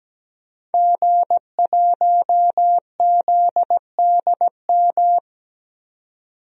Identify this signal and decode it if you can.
Morse code, heard as a keyed tone, about 17 wpm, 700 Hz. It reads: G1ZDM
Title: DXCC-100 Call Signs - 5 Characters - 17wpm